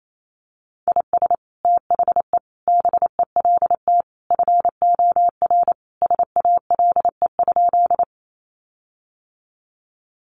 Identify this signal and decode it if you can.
Morse code, heard as a keyed tone, about 28 wpm, 700 Hz. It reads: IS THE BELT FOR SALE?
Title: Sentences from Top 1400 Words 28wpm